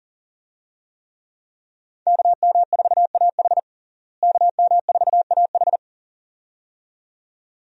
Morse code, heard as a keyed tone, about 40 wpm, 700 Hz. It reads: KM4AH KM4AH
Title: USA Call Signs - 5 Characters - Repeated 2x - 40wpm